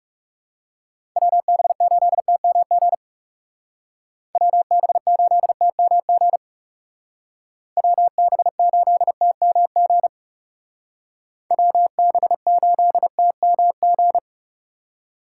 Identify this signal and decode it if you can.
Morse code, heard as a keyed tone, about 37 wpm, 700 Hz. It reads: WB8TMG WB8TMG WB8TMG WB8TMG
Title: Speed Racing - USA 2x3 Call Signs 30wpm